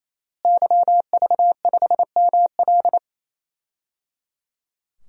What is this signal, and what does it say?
Morse code, heard as a keyed tone, about 28 wpm, 700 Hz. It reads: YV5ML
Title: DXCC-100 Call Signs - 5 Characters - Repeated 3x - 28wpm